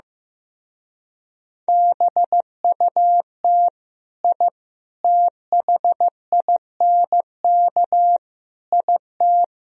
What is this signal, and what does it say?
Morse code, heard as a keyed tone, about 15 wpm, 700 Hz. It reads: BUT I THINK IT
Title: Speed-Racing - Sets of 4 Words 15wpm